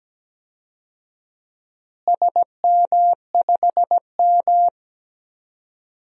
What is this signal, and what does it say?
Morse code, heard as a keyed tone, about 17 wpm, 700 Hz. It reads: SM5M
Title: DXCC-100 Call Signs - 4 Characters - Repeated 3x - 17wpm